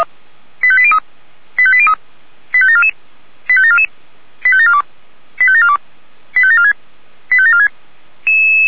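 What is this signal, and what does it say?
Five-tone selective call (ZVEI), heard as quick runs of five tones, tones of about 70 ms. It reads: 85803 85803 85744 85744 85742 85742 85746 85746
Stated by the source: ZVEI five-tone call received by SDR.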